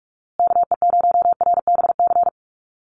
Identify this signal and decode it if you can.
Morse code, heard as a keyed tone, about 44 wpm, 700 Hz. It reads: KE0RBC
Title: USA Call Signs Repeated 2x 45wpm